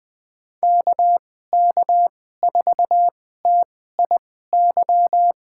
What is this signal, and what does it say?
Morse code, heard as a keyed tone, about 21 wpm, spaced as for 15 wpm, 700 Hz. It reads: KK4TIY
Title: USA Call Signs - 6 Characters - 2x Character Spacing - 20wpm